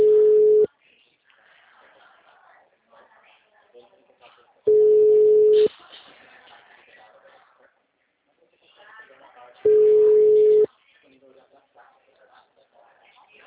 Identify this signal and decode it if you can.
Ringback tone (Europe, 425 Hz), with speech.